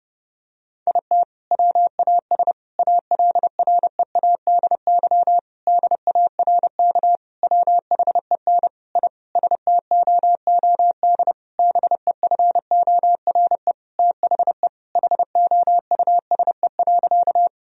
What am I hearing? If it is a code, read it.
Morse code, heard as a keyed tone, about 30 wpm, 700 Hz. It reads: IT WAS ALREADY DARK WHEN I STOOD BEFORE THE HOUSE.
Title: Sentences from Top 1000 Words - Review Part 2 - 30wpm